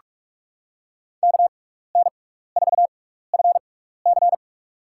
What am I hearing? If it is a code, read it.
Morse code, heard as a keyed tone, about 44 wpm, spaced as for 15 wpm, 700 Hz. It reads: KN4FC
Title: USA Call Signs - 5 Characters - 6x Character Spacing - 45wpm